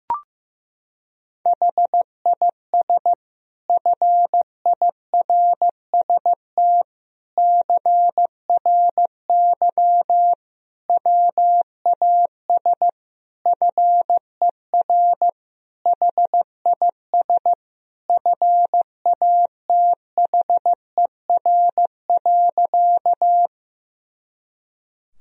Morse code, heard as a keyed tone, about 16 wpm, 700 Hz. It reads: HIS FIRST CRY WAS FER HIS FATHER.
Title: Sentences from Top 500 Words and Abbreviations 15wpm